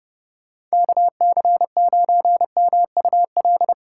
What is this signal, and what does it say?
Morse code, heard as a keyed tone, about 30 wpm, 700 Hz. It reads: KC9MUL
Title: Speed Racing - USA 2x3 Call Signs 25wpm